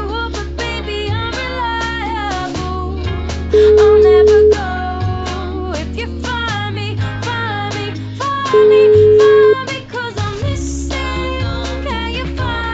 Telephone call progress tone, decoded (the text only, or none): ringback tone